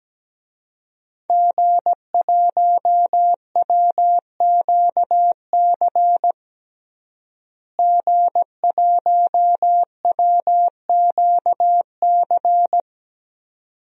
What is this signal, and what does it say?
Morse code, heard as a keyed tone, about 17 wpm, 700 Hz. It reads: G1WQC G1WQC
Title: DXCC-100 Call Signs - 5 Characters - Repeated 2x - 17wpm